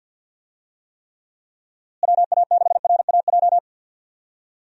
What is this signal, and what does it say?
Morse code, heard as a keyed tone, about 52 wpm, 700 Hz. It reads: WA6RAJ
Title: USA Call Signs - 6 Characters - Repeated 2x - 50wpm